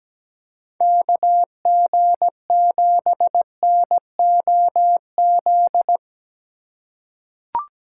Morse code, heard as a keyed tone, about 17 wpm, 700 Hz. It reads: KG7NOZ
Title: USA Call Signs - 6 Characters - 4x Character Spacing - 17wpm